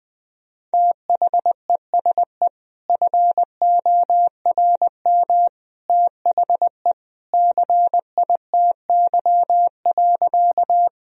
Morse code, heard as a keyed tone, about 20 wpm, 700 Hz. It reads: THESE FORM THE CITY.
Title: Sentences from Top 1000 Words - Review Part 1 - 20wpm